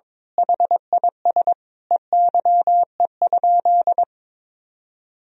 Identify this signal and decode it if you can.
Morse code, heard as a keyed tone, about 22 wpm, 700 Hz. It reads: HIS EYE?
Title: Sentences from Top 300 Words and Abbreviations 22wpm